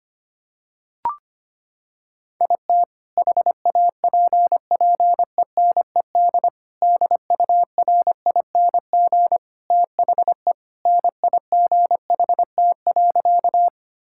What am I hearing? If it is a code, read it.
Morse code, heard as a keyed tone, about 25 wpm, 700 Hz. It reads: IT HAPPENED DURING THE NIGHT.
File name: practice / 25wpm / Sentences with 2-Word Phrases - 25wpm